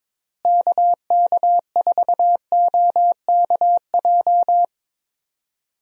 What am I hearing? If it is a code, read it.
Morse code, heard as a keyed tone, about 22 wpm, 700 Hz. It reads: KK4OKJ
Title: USA Call Signs - 6 Characters - Repeated 2x - 22wpm